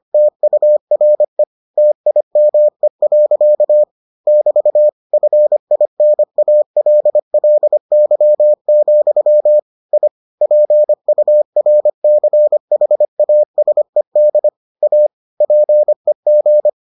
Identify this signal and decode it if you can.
Morse code, heard as a keyed tone, about 25 wpm, 600 Hz. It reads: TURE TIME. <BT> FINALLY, I PURCHASED A PEG